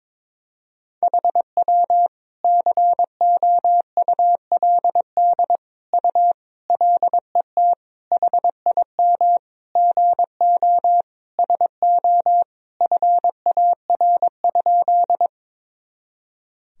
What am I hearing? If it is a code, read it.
Morse code, heard as a keyed tone, about 22 wpm, 700 Hz. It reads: HW COULD U LET HIM GO SO FAR?
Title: Sentences from Top 300 Words and Abbreviations 22wpm